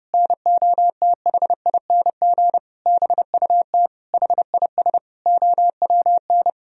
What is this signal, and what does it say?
Morse code, heard as a keyed tone, about 32 wpm, 700 Hz. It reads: NOTHING BUT HIS OWN
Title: Sentences from Top 1000 Words - Review Part 4 - 30wpm